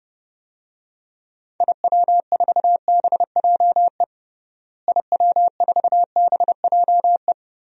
Morse code, heard as a keyed tone, about 30 wpm, 700 Hz. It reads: IW4BJE IW4BJE
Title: DXCC-100 Call Signs - 6 Characters - Repeated 2x - 30wpm